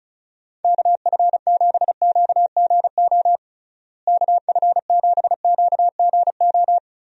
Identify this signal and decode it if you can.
Morse code, heard as a keyed tone, about 34 wpm, 700 Hz. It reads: KF7QGO KF7QGO
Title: USA Call Signs - 6 Characters - Repeated 2x - 35wpm